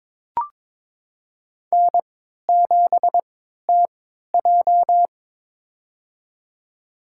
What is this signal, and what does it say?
Morse code, heard as a keyed tone, about 21 wpm, spaced as for 12 wpm, 700 Hz. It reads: N7TJ
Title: USA Call Signs - 4 Characters - 3x Character Spacing - 22wpm